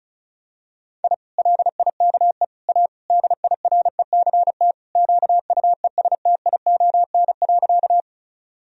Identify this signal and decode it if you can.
Morse code, heard as a keyed tone, about 34 wpm, 700 Hz. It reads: I LIKE A DIRECT QUESTION.
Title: Sentences from Top 1000 Words - Review Part 2 - 35wpm